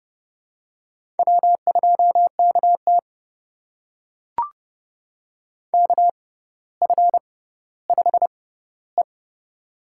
Morse code, heard as a keyed tone, about 30 wpm, spaced as for 15 wpm, 700 Hz. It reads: W2KT KF5E
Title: USA Call Signs - 4 Characters - 6x Character Spacing - 30wpm